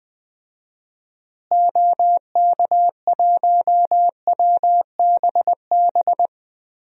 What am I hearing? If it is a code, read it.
Morse code, heard as a keyed tone, about 20 wpm, 700 Hz. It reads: OK1WBB
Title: DXCC-100 Call Signs - 6 Characters - Repeated 2x - 20wpm